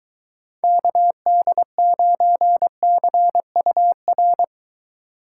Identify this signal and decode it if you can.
Morse code, heard as a keyed tone, about 23 wpm, 700 Hz. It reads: KD9CUR